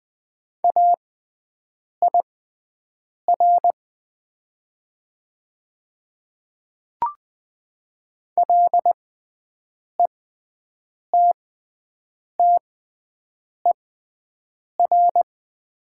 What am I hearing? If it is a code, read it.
Morse code, heard as a keyed tone, about 20 wpm, spaced as for 7 wpm, 700 Hz. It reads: AIR LETTER
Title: 6x Character Spacing - Top 300 Words - 20wpm